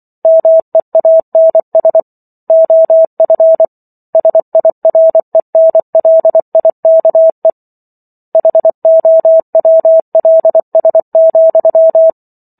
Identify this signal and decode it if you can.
Morse code, heard as a keyed tone, about 24 wpm, 650 Hz. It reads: MEANS OF SIRENLIKE HOWLS,